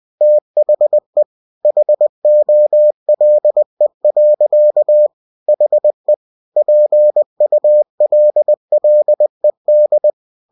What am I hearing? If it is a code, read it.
Morse code, heard as a keyed tone, about 20 wpm, 600 Hz. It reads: THE HOLE. HE PULLED
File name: cw-books / winnie-the-pooh / winnie-20wpm-page-0016